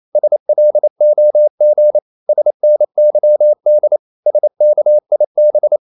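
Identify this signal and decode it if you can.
Morse code, heard as a keyed tone, about 28 wpm, 600 Hz. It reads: SLOG SNYD SKIB
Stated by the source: Danish short words sent in Morse at 28 wpm, from CW med Gnister.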